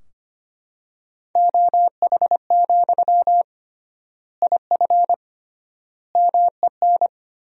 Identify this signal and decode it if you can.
Morse code, heard as a keyed tone, about 25 wpm, 700 Hz. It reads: OH, IF MEN
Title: 3x Word-Spacing - Sentences from Top 500 Words - 25wpm